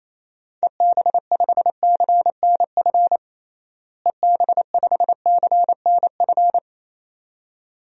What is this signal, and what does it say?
Morse code, heard as a keyed tone, about 28 wpm, 700 Hz. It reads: EB5CNF EB5CNF
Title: DXCC-100 Call Signs - 6 Characters - Repeated 2x - 28wpm